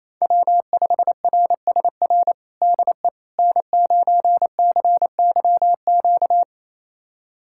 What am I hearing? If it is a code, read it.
Morse code, heard as a keyed tone, about 28 wpm, 700 Hz. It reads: W5RSR DE N9CYQ
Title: USA Callsign DE Callsign - 5 Characters - 28wpm